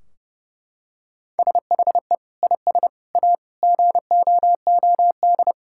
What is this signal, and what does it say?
Morse code, heard as a keyed tone, about 30 wpm, 700 Hz. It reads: SHE IS A GOOD
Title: Sentences from Top 1600 Words 30wpm